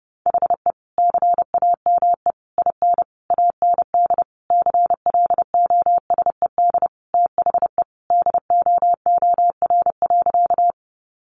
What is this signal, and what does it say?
Morse code, heard as a keyed tone, about 30 wpm, 700 Hz. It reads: HE CAME IN AND CLOSED THE DOOR.
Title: Sentences from Top 1400 Words 30wpm